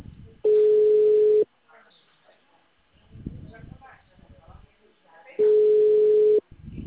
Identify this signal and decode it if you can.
Ringback tone (Europe, 425 Hz), with speech.
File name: out- -2013- - - .28779